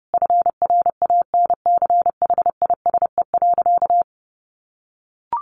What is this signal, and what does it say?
Morse code, heard as a keyed tone, about 30 wpm, 700 Hz. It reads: FRANCHISE.
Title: A Sound Of Thunder 30wpm